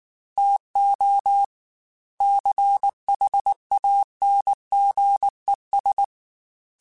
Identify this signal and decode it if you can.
Morse code, heard as a keyed tone, about 19 wpm, 800 Hz. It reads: TO CHANGES
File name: Technical Subjects / Technical Subjects-mp3s / Reactance @18